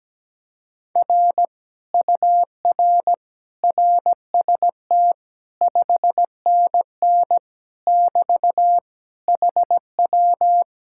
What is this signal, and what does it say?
Morse code, heard as a keyed tone, about 17 wpm, 700 Hz. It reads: R UR RST 5NN <BT> HW
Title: Speed-Racing Common QSO Phrases - 17wpm